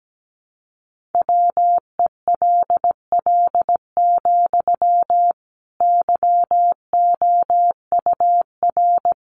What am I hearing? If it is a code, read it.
Morse code, heard as a keyed tone, about 17 wpm, 700 Hz. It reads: WELL, YOUR
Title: Sentences from Top 1400 Words 17wpm